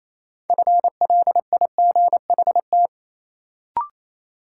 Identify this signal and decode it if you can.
Morse code, heard as a keyed tone, about 28 wpm, 700 Hz. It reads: FLIGHT
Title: Sentences from Top 1000 Words - New Word Drill - 28wpm